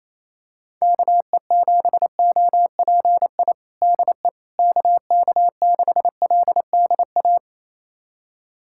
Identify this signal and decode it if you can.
Morse code, heard as a keyed tone, about 27 wpm, 700 Hz. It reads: KE7OPI DE KK6LDA
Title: USA Callsign DE Callsign - 6 Characters - 28wpm